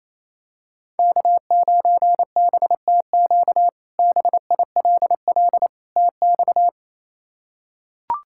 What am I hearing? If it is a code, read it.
Morse code, heard as a keyed tone, about 28 wpm, 700 Hz. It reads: K9BTQ BILL TX
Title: K1USN SST QSO Practice - 28wpm